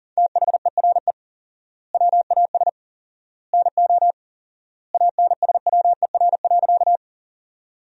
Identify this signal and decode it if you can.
Morse code, heard as a keyed tone, about 40 wpm, 700 Hz. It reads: THERE WAS NO ANSWER.
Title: 4x Word-Spacing - Sentences from Top 300 Words - 40wpm